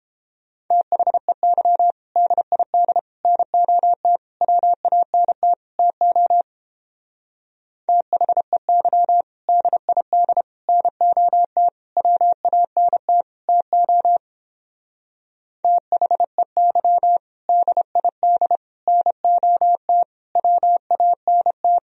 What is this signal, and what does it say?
Morse code, heard as a keyed tone, about 31 wpm, 700 Hz. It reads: THEY DID NOT WANT TO THEY DID NOT WANT TO THEY DID NOT WANT